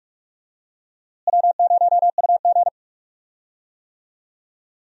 Morse code, heard as a keyed tone, about 43 wpm, 700 Hz. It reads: W0UG
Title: USA Call Signs - 4 Characters - Repeated 3x - 45wpm